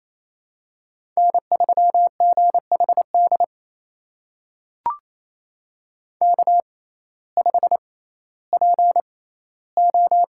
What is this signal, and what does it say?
Morse code, heard as a keyed tone, about 27 wpm, spaced as for 15 wpm, 700 Hz. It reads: N3GHD K5PO
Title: USA Call Signs - 5 Characters - 6x Character Spacing - 28wpm